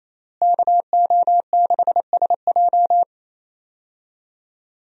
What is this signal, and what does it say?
Morse code, heard as a keyed tone, about 28 wpm, 700 Hz. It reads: KO6SJ
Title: USA Call Signs - 5 Characters - Repeated 2x - 28wpm